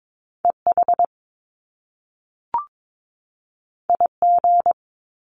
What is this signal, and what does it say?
Morse code, heard as a keyed tone, about 22 wpm, 700 Hz. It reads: EH IG
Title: Top Two Letter Combinations in Words 22wpm